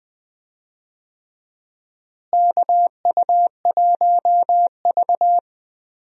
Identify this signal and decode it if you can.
Morse code, heard as a keyed tone, about 20 wpm, 700 Hz. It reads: KU1V
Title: USA Call Signs - 4 Characters - Repeated 3x - 20wpm